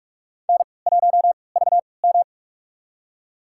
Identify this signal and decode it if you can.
Morse code, heard as a keyed tone, about 47 wpm, spaced as for 26 wpm, 700 Hz. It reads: N1VM
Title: USA Call Signs - 4 Characters - 3x Character Spacing - 45wpm